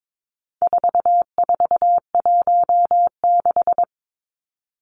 Morse code, heard as a keyed tone, about 22 wpm, 700 Hz. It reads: 4416
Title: Numbers - 4-Digit 22wpm